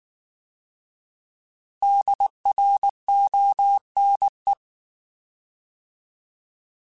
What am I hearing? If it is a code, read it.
Morse code, heard as a keyed tone, about 19 wpm, 800 Hz. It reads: DRONE